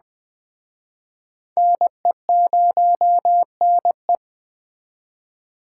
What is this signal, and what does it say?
Morse code, heard as a keyed tone, about 21 wpm, 700 Hz. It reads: NE0NE
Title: Speed Racing - USA Call Signs 17wpm